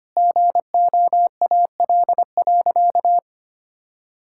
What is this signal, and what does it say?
Morse code, heard as a keyed tone, about 25 wpm, 700 Hz. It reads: GOAL.
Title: Sentences from Top 1000 Words - Review Part 5 - 25wpm